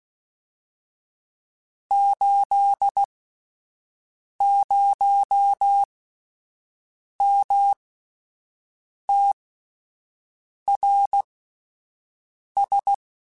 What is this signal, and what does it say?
Morse code, heard as a keyed tone, about 16 wpm, spaced as for 5 wpm, 800 Hz. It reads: 80MTRS
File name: (spaced) / QSOs Spaced-mp3s / WD0FBT 1 @10 sp